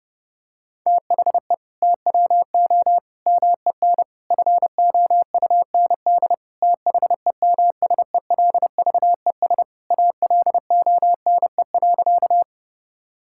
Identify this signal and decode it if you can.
Morse code, heard as a keyed tone, about 30 wpm, 700 Hz. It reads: THE TWO MEN FOUND THEMSELVES ALONE.
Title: Sentences from Top 1000 Words - Review Part 5 - 30wpm